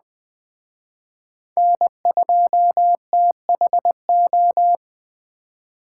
Morse code, heard as a keyed tone, about 21 wpm, 700 Hz. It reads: N2THO
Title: Speed Racing - USA Call Signs 17wpm